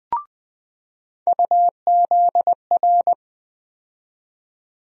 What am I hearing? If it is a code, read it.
Morse code, heard as a keyed tone, about 20 wpm, 700 Hz. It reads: UZR
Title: USA Call Sign Postfixes - 3 Letters 20wpm